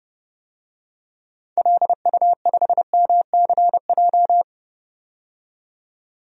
Morse code, heard as a keyed tone, about 30 wpm, 700 Hz. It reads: LU5MCJ
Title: DXCC-100 Call Signs - 6 Characters - 30wpm